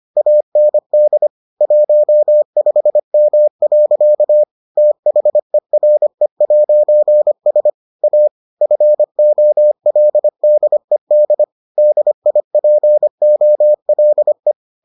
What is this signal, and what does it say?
Morse code, heard as a keyed tone, about 25 wpm, 600 Hz. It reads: AND 15M. THERE'S A FOLDED DIPOLE